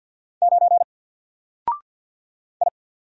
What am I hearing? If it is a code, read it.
Morse code, heard as a keyed tone, about 50 wpm, 700 Hz. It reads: 9 I
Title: Single Letter-Number 50wpm